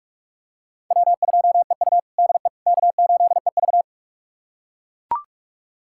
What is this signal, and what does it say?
Morse code, heard as a keyed tone, about 41 wpm, 700 Hz. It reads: W2EU DE K8EV
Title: USA Callsign DE Callsign - 4 Characters - 45wpm